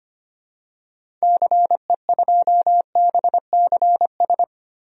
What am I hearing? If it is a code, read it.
Morse code, heard as a keyed tone, about 25 wpm, 700 Hz. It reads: CE2BCS
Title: DXCC-100 Call Signs - 6 Characters - Repeated 2x - 25wpm